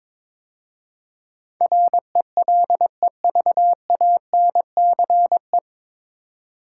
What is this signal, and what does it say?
Morse code, heard as a keyed tone, about 23 wpm, 700 Hz. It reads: RELEVANCE
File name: 9 Letter Words - First 3 Letters Given - 22wpm